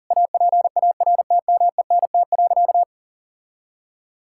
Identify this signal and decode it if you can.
Morse code, heard as a keyed tone, about 40 wpm, 700 Hz. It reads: APARTMENT.
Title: Sentences from Top 1000 Words - Review Part 6 - 40wpm